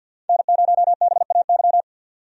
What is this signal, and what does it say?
Morse code, heard as a keyed tone, about 45 wpm, 700 Hz. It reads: N0BAY
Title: USA Call Signs - 5 Characters - 50wpm